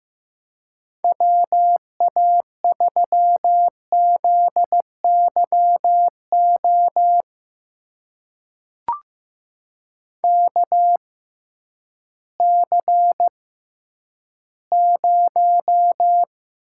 Morse code, heard as a keyed tone, about 15 wpm, spaced as for 10 wpm, 700 Hz. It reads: WA3ZYO KC0
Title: USA Call Signs - 6 Characters - 6x Character Spacing - 15wpm